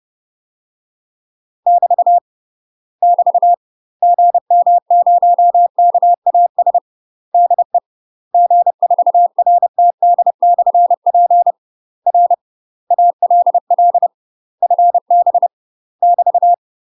Morse code, heard as a keyed tone, about 30 wpm, 700 Hz. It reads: <BT> <BT> GM0KAS DE G4RTD/P R ALL FB <BT>
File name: UK MASTER QSOs TRAINING / 30WPM / UK MASTER QSOs 0008